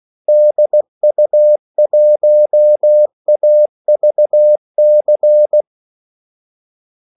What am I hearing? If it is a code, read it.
Morse code, heard as a keyed tone, about 16 wpm, 600 Hz. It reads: DU1AVC
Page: Kaldesignaler 16 wpm